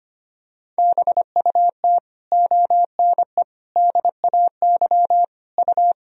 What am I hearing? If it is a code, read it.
Morse code, heard as a keyed tone, about 25 wpm, 700 Hz. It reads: BUT ONE DAY U
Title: Sentences from Top 400 Words and Abbreviations 25wpm